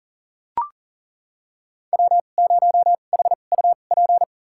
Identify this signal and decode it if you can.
Morse code, heard as a keyed tone, about 40 wpm, spaced as for 29 wpm, 700 Hz. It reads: W0HUP
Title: USA Call Signs - 5 Characters - 2x Character Spacing - 40wpm